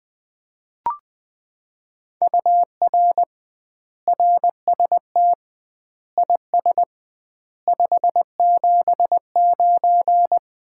Morse code, heard as a keyed tone, about 20 wpm, 700 Hz. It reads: UR RST IS 579
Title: Common QSO Phrases - 2x Word-Spacing - 20wpm